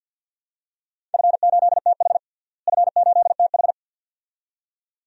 Morse code, heard as a keyed tone, about 48 wpm, 700 Hz. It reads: F8TH F8TH
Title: DXCC-100 Call Signs - 4 Characters - Repeated 2x - 50wpm